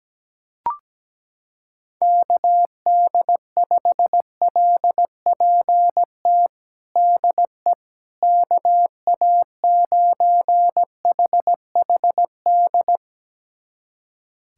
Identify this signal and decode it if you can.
Morse code, heard as a keyed tone, about 17 wpm, 700 Hz. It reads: KD5LPT DE KA9HHD
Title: USA Callsign DE Callsign 17wpm